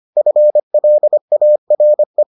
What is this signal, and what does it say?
Morse code, heard as a keyed tone, about 25 wpm, 600 Hz. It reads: FLARE